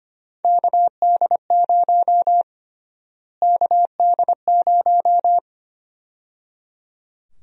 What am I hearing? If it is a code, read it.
Morse code, heard as a keyed tone, about 25 wpm, 700 Hz. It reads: KD0 KD0
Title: USA Call Sign Prefixes - Repeated 2x - 25wpm